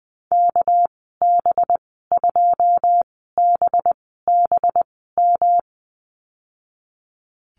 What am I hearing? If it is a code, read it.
Morse code, heard as a keyed tone, about 20 wpm, spaced as for 15 wpm, 700 Hz. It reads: KB2BBM
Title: 219 Lesson 35g - Number 0 - Call Signs 20wpm